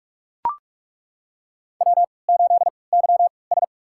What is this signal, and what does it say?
Morse code, heard as a keyed tone, about 42 wpm, spaced as for 25 wpm, 700 Hz. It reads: W8YS
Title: USA Call Signs - 4 Characters - 3x Character Spacing - 45wpm